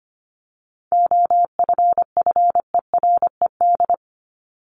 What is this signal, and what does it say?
Morse code, heard as a keyed tone, about 25 wpm, 700 Hz. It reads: OFFERED
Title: ED Suffix - Top 100 Words with Base Words 25wpm